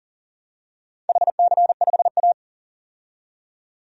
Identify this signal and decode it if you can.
Morse code, heard as a keyed tone, about 37 wpm, 700 Hz. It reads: HC5A
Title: DXCC-100 Call Signs - 4 Characters - 40wpm